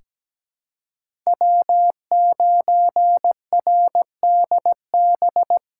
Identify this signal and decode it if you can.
Morse code, heard as a keyed tone, about 17 wpm, 700 Hz. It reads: W9RDB
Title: USA Callsign DE Callsign - 5 Characters - 17wpm